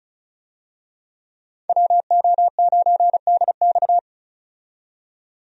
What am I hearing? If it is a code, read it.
Morse code, heard as a keyed tone, about 36 wpm, 700 Hz. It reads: WO9DX
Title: USA Call Signs - 5 Characters - 35wpm